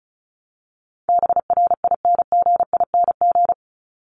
Morse code, heard as a keyed tone, about 34 wpm, 700 Hz. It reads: BRINGING
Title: ING Suffix - Top 100 Words with Base Words 35wpm